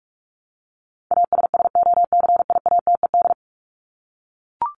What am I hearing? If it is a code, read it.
Morse code, heard as a keyed tone, about 44 wpm, 700 Hz. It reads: ASSOCIATED
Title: ED Suffix - Top 100 Words with Base Words 45wpm